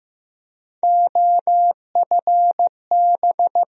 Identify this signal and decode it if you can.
Morse code, heard as a keyed tone, about 16 wpm, 700 Hz. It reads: OFB
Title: USA Call Sign Postfixes - 3 Letters 15wpm